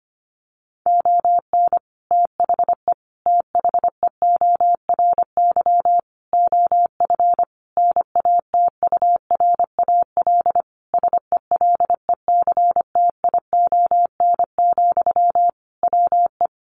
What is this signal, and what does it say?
Morse code, heard as a keyed tone, about 25 wpm, 700 Hz. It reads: ON THE THEORY OF NATURAL SELECTION, WE
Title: Sentences from Top 1400 Words 25wpm